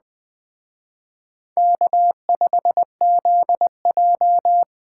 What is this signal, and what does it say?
Morse code, heard as a keyed tone, about 20 wpm, 700 Hz. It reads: K5ZJ
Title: Speed Racing - USA 1x2 Call Signs 17wpm